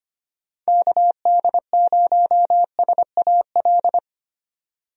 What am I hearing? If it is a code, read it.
Morse code, heard as a keyed tone, about 25 wpm, 700 Hz. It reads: KD0SAL
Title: USA Call Signs - 6 Characters - 25wpm